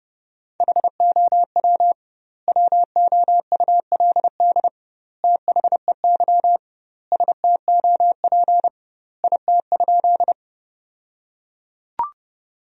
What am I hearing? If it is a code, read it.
Morse code, heard as a keyed tone, about 30 wpm, 700 Hz. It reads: HOW WOULD THEY STOP IT?
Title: 2x Word-Spacing - Sentences from Top 300 Words - 30wpm